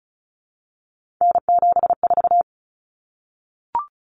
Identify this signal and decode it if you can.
Morse code, heard as a keyed tone, about 34 wpm, 700 Hz. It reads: N74
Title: Common Cut Numbers - 3-Digit 35wpm